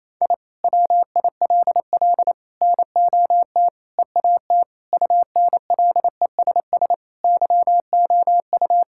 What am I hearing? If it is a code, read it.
Morse code, heard as a keyed tone, about 28 wpm, 700 Hz. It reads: I WILL NOT EAT UNLESS YOU
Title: Sentences from Top 1000 Words - Review Part 6 - 28wpm